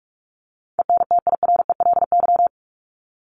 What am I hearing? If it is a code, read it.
Morse code, heard as a keyed tone, about 44 wpm, 700 Hz. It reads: ENTIRELY
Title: LY Suffix - Top 100 Words with Base Words 45wpm